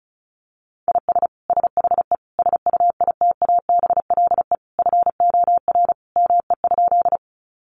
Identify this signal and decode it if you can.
Morse code, heard as a keyed tone, about 34 wpm, 700 Hz. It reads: IS SHE SUITABLE FOR ME?
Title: Sentences from Top 1400 Words 35wpm